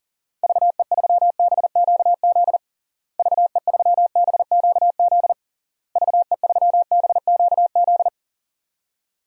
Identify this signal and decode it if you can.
Morse code, heard as a keyed tone, about 40 wpm, 700 Hz. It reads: VE3BQZ VE3BQZ VE3BQZ
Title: DXCC-100 Call Signs - 6 Characters - Repeated 3x - 40wpm